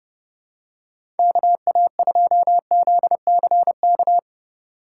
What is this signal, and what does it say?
Morse code, heard as a keyed tone, about 30 wpm, 700 Hz. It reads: KA2ZCK
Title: USA Call Signs - 6 Characters - 3x Character Spacing - 30wpm